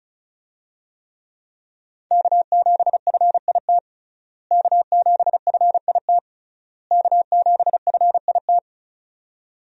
Morse code, heard as a keyed tone, about 36 wpm, 700 Hz. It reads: K7FIT K7FIT K7FIT
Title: USA Call Signs - 5 Characters - Repeated 3x - 35wpm